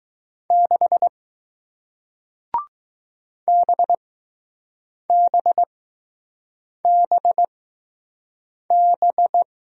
Morse code, heard as a keyed tone, about 20 wpm, spaced as for 6 wpm, 700 Hz. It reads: 6 BBBB